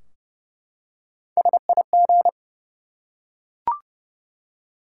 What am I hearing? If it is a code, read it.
Morse code, heard as a keyed tone, about 30 wpm, 700 Hz. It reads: SIG
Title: Top 90 QSO Elements 30wpm